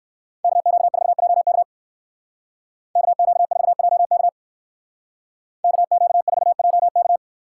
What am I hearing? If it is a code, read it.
Morse code, heard as a keyed tone, about 56 wpm, 700 Hz. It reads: KQ4JK KQ4JK KQ4JK
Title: Speed Racing - USA 2x2 Call Signs 45wpm